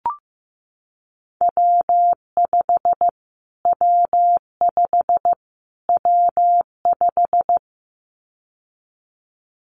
Morse code, heard as a keyed tone, about 15 wpm, 700 Hz. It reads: W5 W5 W5
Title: Callsign Prefixes for Top 25 DXCC - Repeated 3x - 15wpm